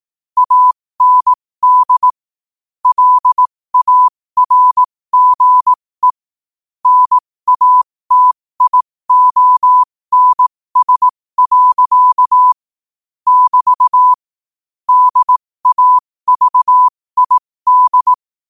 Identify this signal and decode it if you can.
Morse code, heard as a keyed tone, about 18 wpm, spaced as for 15 wpm, 1000 Hz. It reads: AND LARGE NATIONS. <BT> DAVID
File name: qotdfiles / 15wpm